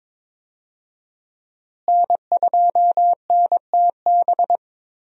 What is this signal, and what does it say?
Morse code, heard as a keyed tone, about 22 wpm, 700 Hz. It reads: N2NTB
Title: USA Call Signs - 5 Characters - Repeated 3x - 22wpm